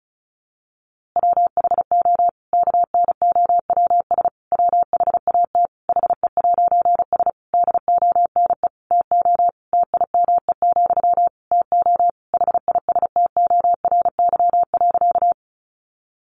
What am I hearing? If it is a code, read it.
Morse code, heard as a keyed tone, about 34 wpm, 700 Hz. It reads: WHO KNOWS WHAT HE'S DONE TO TIME, TO HISTORY.
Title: A Sound Of Thunder 35wpm